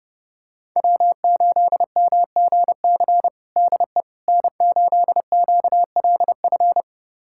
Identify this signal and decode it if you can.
Morse code, heard as a keyed tone, about 30 wpm, 700 Hz. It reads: W8MGC DE N8QLF
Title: USA Callsign DE Callsign - 5 Characters - 30wpm